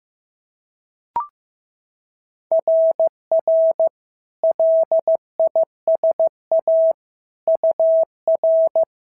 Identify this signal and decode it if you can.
Morse code, heard as a keyed tone, about 14 wpm, 650 Hz. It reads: RR LISA UR
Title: SOTA QSO Practice 15wpm